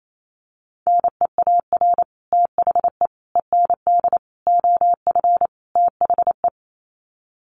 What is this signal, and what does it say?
Morse code, heard as a keyed tone, about 28 wpm, 700 Hz. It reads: NEAR THE END OF THE
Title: Sets of 5 Words Encore 28wpm